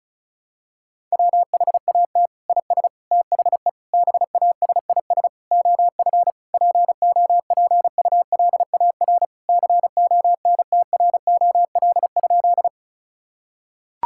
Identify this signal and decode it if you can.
Morse code, heard as a keyed tone, about 36 wpm, 700 Hz. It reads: WHAT IS THE BASIS OF POPULAR CONTROL?
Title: Sentences from Top 1000 Words - Review Part 5 - 35wpm